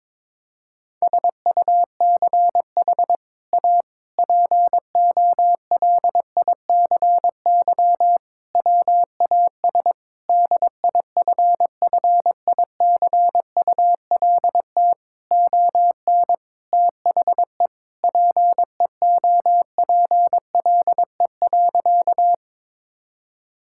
Morse code, heard as a keyed tone, about 22 wpm, 700 Hz. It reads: SUCH A POLICY WAS DIFFICULT ON THE PEOPLE.
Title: Sentences from Top 1000 Words - Review Part 5 - 22wpm